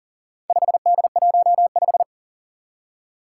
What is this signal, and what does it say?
Morse code, heard as a keyed tone, about 43 wpm, 700 Hz. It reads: 5D15
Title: Four Challenging Letter-Number Combinations 40wpm